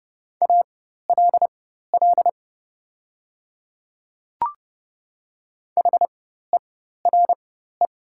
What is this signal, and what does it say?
Morse code, heard as a keyed tone, about 30 wpm, spaced as for 14 wpm, 700 Hz. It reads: ALL HERE
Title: 4x Character Spacing - Top 200 Words - 30wpm